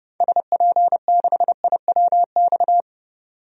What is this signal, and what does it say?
Morse code, heard as a keyed tone, about 30 wpm, 700 Hz. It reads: SP6IWX
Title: DXCC-100 Call Signs - 6 Characters - 30wpm